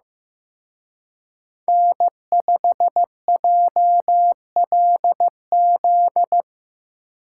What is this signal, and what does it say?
Morse code, heard as a keyed tone, about 18 wpm, 700 Hz. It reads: N5JLZ
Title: Speed Racing - USA 1x3 Call Signs 15wpm